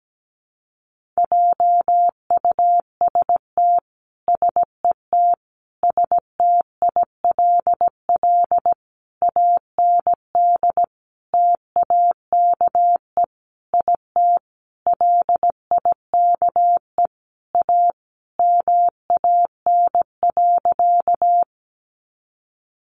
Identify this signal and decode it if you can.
Morse code, heard as a keyed tone, about 17 wpm, 700 Hz. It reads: JUST SET STILL AND TAKE IT LIKE A MAN.
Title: Sentences from Top 300 Words 17wpm